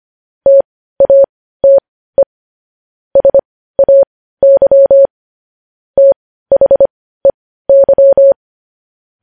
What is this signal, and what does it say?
Morse code, heard as a keyed tone, about 25 wpm, spaced as for 15 wpm, 550 Hz. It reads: TATE SAY THEY
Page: News Headlines in Morse Code at 15 WPM